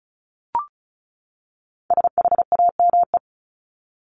Morse code, heard as a keyed tone, about 34 wpm, 700 Hz. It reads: SHAME
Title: Top 1751-2000 Words 35wpm